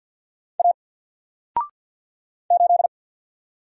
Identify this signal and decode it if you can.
Morse code, heard as a keyed tone, about 48 wpm, 700 Hz. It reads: A 8
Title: Single Letter-Number Alt 50wpm